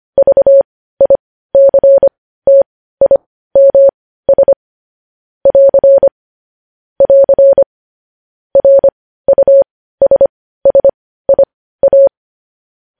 Morse code, heard as a keyed tone, about 25 wpm, spaced as for 15 wpm, 550 Hz. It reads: VICTIMS <AR> <AR> RUSSIA
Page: news headlines in morse code at 15 wpm